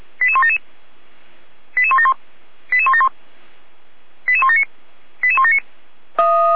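Five-tone selective call (ZVEI), heard as quick runs of five tones, tones of about 70 ms. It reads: 88177 88171 88171 88179 88179
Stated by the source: ZVEI five-tone selective call received on SDR.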